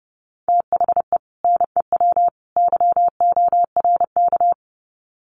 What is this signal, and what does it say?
Morse code, heard as a keyed tone, about 29 wpm, 700 Hz. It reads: THE NEW YORK
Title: Sets of 3 Words 30wpm